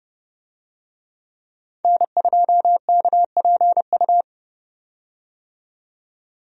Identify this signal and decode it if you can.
Morse code, heard as a keyed tone, about 30 wpm, 700 Hz. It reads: N2KPU
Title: USA Call Signs - 5 Characters - 30wpm